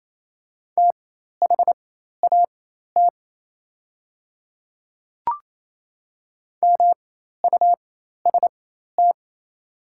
Morse code, heard as a keyed tone, about 28 wpm, spaced as for 13 wpm, 700 Hz. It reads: THAT MUST
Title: 4x Character Spacing - Top 200 Words - 28wpm